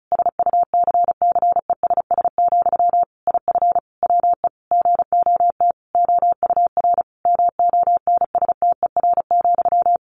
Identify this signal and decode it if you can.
Morse code, heard as a keyed tone, about 34 wpm, 700 Hz. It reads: SUCCESS, IF WE GOT OUR MONSTER,
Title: A Sound Of Thunder 35wpm